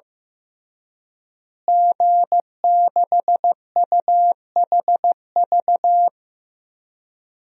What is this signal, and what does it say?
Morse code, heard as a keyed tone, about 16 wpm, 700 Hz. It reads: G6UHV
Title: DXCC-100 Call Signs - 5 Characters - Repeated 3x - 15wpm